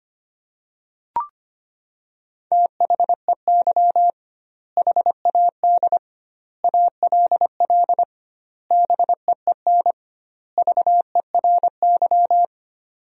Morse code, heard as a keyed tone, about 25 wpm, 700 Hz. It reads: THEY HAD ALL BEEN VERY
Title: 2x Word-Spacing - Sentences from Top 200 Words - 25wpm